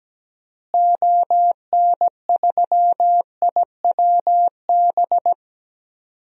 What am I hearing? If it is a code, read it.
Morse code, heard as a keyed tone, about 17 wpm, 700 Hz. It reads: ON3IWB
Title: DXCC-100 Call Signs - 6 Characters - Repeated 3x - 17wpm